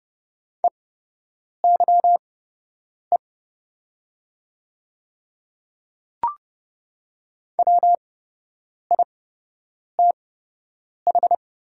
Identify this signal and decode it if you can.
Morse code, heard as a keyed tone, about 30 wpm, spaced as for 8 wpm, 700 Hz. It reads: EYE WITH
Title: 8x Character Spacing - Top 300 Words - 30wpm